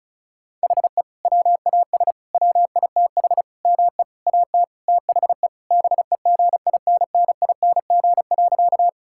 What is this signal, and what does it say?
Morse code, heard as a keyed tone, about 36 wpm, 700 Hz. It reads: HE WAS WITH ME AT THE BEGINNING.
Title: Sentences from Top 1000 Words - Review Part 5 - 35wpm